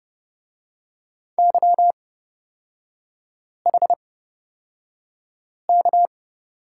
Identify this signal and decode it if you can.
Morse code, heard as a keyed tone, about 31 wpm, spaced as for 5 wpm, 700 Hz. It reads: YHK
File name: Lesson 37e - Letter x - Three Characters 30wpm